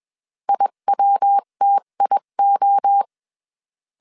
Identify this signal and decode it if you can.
Morse code, heard as a keyed tone, about 21 wpm, 800 Hz. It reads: IWTIO
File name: 20wpm